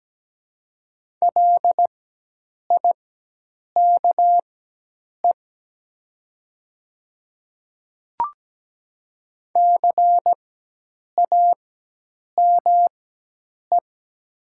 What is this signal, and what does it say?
Morse code, heard as a keyed tone, about 17 wpm, spaced as for 8 wpm, 700 Hz. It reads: LIKE CAME
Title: 4x Character Spacing - Top 200 Words - 17wpm